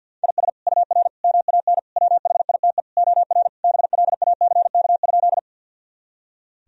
Morse code, heard as a keyed tone, about 52 wpm, 700 Hz. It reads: IS UR MAN WHITE OR BLACK?
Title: Sentences from Top 400 Words and Abbreviations 50wpm